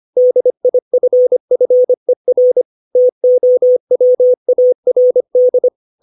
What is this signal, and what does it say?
Morse code, heard as a keyed tone, about 25 wpm, 500 Hz. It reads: DIFFER TOWARD
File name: 632 words / 632 Words 25WPM